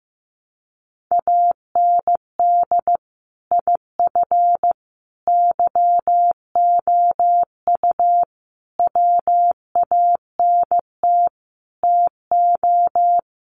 Morse code, heard as a keyed tone, about 15 wpm, 700 Hz. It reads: AND IF YOU WANT TO
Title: Sets of 5 Words Encore 15wpm